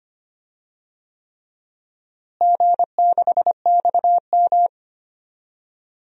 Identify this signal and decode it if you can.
Morse code, heard as a keyed tone, about 25 wpm, 700 Hz. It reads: G6XM
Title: DXCC-100 Call Signs - 4 Characters - Repeated 3x - 25wpm